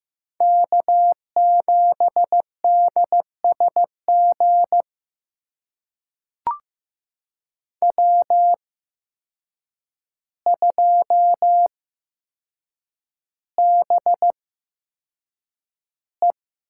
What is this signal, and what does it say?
Morse code, heard as a keyed tone, about 15 wpm, spaced as for 7 wpm, 700 Hz. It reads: K7DSG W2BE
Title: USA Call Signs - 5 Characters - 8x Character Spacing - 15wpm